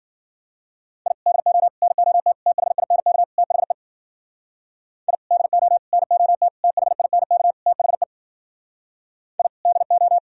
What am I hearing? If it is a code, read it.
Morse code, heard as a keyed tone, about 54 wpm, 700 Hz. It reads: I DO NOT THINK THE I DO NOT THINK THE I DO